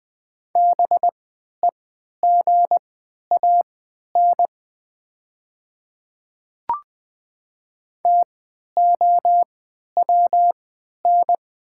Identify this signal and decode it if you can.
Morse code, heard as a keyed tone, about 20 wpm, spaced as for 11 wpm, 700 Hz. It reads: BEGAN TOWN
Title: 3x Character Spacing - Top 500 Words - 20wpm